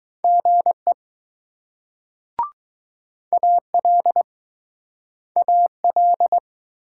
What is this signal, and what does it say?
Morse code, heard as a keyed tone, about 20 wpm, 700 Hz. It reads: GE AL AL